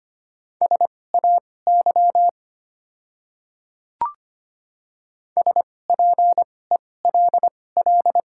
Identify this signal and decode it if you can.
Morse code, heard as a keyed tone, about 25 wpm, spaced as for 18 wpm, 700 Hz. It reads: SAY SPELL
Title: 2x Character Spacing - Top 300 Words - 25wpm